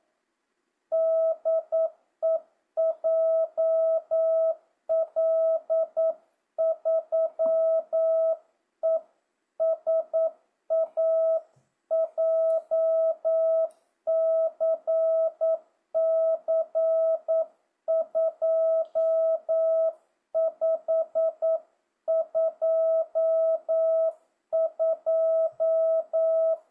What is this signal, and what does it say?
Morse code, heard as a keyed tone, about 9 wpm, 650 Hz. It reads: DE JL3ESA JCC 2522
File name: jh8cbh / 2025cw